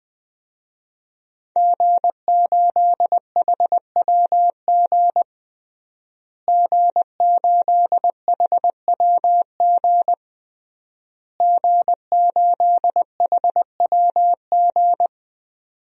Morse code, heard as a keyed tone, about 20 wpm, 700 Hz. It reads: G8HWG G8HWG G8HWG
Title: DXCC-100 Call Signs - 5 Characters - Repeated 3x - 20wpm